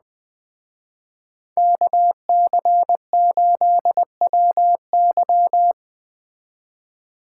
Morse code, heard as a keyed tone, about 21 wpm, 700 Hz. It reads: KC8WY
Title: Speed Racing - USA 2x2 Call Signs 17wpm